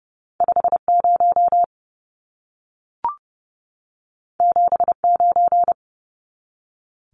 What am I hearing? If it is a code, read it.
Morse code, heard as a keyed tone, about 30 wpm, 700 Hz. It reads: 50 79
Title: Numbers - 2-Digit 30wpm